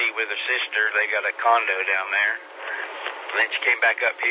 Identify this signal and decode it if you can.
Emergency Alert System (SAME) tones, heard as harsh buzzing bursts, with speech. It is heard from 0.7 to 4.0 s.